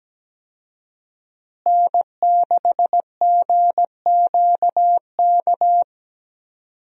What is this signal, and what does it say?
Morse code, heard as a keyed tone, about 17 wpm, 700 Hz. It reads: N6GQK
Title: USA Call Signs - 5 Characters - Repeated 2x - 17wpm